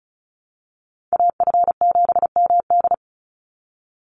Morse code, heard as a keyed tone, about 34 wpm, 700 Hz. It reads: AF7MD
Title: NA SOTA Chaser Callsigns 35wpm